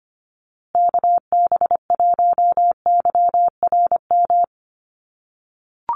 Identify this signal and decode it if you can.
Morse code, heard as a keyed tone, about 24 wpm, 700 Hz. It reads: KB1YRM
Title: USA Call Signs Repeated 2x 25wpm